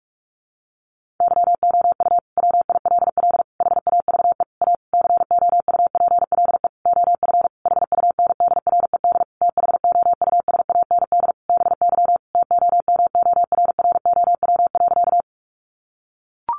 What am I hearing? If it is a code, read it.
Morse code, heard as a keyed tone, about 44 wpm, 700 Hz. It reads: YOU WILL HAVE A COUPLE OF HUNDRED THOUSAND BY TOMORROW.
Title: Sentences from Top 1100 Words 45wpm